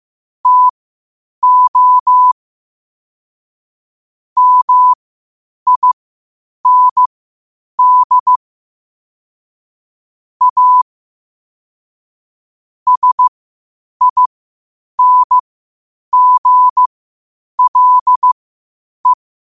Morse code, heard as a keyed tone, about 15 wpm, spaced as for 8 wpm, 1000 Hz. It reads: TO MIND A SINGLE
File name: qotdfiles / 8wpm